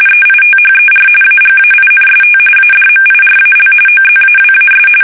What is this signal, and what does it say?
RTTY, about 45.45 baud. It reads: EITHER AGAINST ADMIRAL OZAWAS
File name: fsk850 / battleshipnj